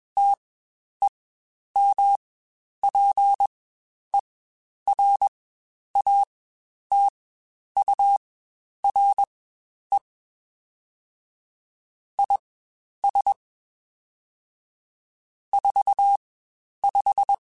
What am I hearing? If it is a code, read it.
Morse code, heard as a keyed tone, about 21 wpm, spaced as for 10 wpm, 800 Hz. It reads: TEMPERATURE IS 45
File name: Spaced Characters / Spaced Characters-mp3s / Weather 3 17 sp @20